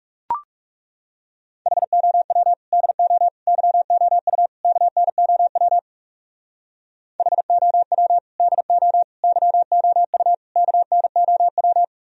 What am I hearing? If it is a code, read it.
Morse code, heard as a keyed tone, about 36 wpm, 700 Hz. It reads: HOW DO YOU KNOW HOW DO YOU KNOW
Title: Speed-Racing - Sets of 4 Words 30wpm